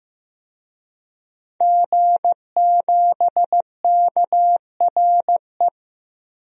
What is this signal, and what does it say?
Morse code, heard as a keyed tone, about 15 wpm, 700 Hz. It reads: G7KRE
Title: DXCC-100 Call Signs - 5 Characters - Repeated 3x - 15wpm